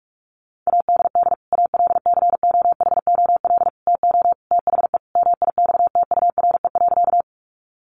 Morse code, heard as a keyed tone, about 44 wpm, 700 Hz. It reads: ADD ALCOHOL TO THE MIXTURE.
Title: Sentences from Top 1500 Words 45wpm